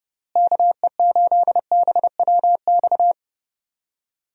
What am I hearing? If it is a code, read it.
Morse code, heard as a keyed tone, about 30 wpm, 700 Hz. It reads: KE8BWX
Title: USA Call Signs - 6 Characters - Repeated 3x - 30wpm